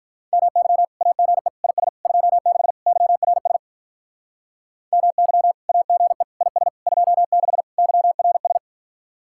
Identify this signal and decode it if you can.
Morse code, heard as a keyed tone, about 44 wpm, 700 Hz. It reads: MY AGE IS 26 YRS MY AGE IS 26 YRS
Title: Speed-Racing Common QSO Phrases - 35wpm